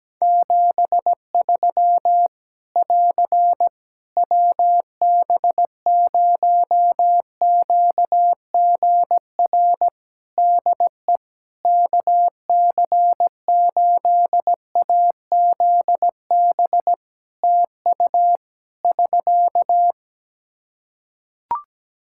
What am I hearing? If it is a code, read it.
Morse code, heard as a keyed tone, about 17 wpm, 700 Hz. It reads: 73 <AR> WB0QGR DE KC8AZB TU <SK>
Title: Practice QSO - Exchange 4 - Ending - 17wpm